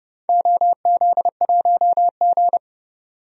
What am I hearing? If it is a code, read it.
Morse code, heard as a keyed tone, about 30 wpm, 700 Hz. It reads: OZ1G
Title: DXCC-100 Call Signs - 4 Characters - 30wpm